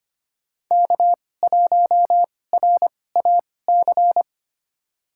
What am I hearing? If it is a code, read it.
Morse code, heard as a keyed tone, about 25 wpm, spaced as for 18 wpm, 700 Hz. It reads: K1RAC
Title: USA Call Signs - 5 Characters - 2x Character Spacing - 25wpm